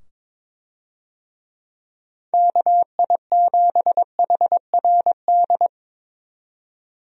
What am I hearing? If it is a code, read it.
Morse code, heard as a keyed tone, about 21 wpm, 700 Hz. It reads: KI7HRD
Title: USA Call Signs - 6 Characters - 22wpm